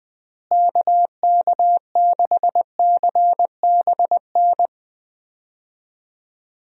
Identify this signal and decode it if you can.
Morse code, heard as a keyed tone, about 20 wpm, 700 Hz. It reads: KK6CBN
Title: USA Call Signs - 6 Characters - Repeated 2x - 20wpm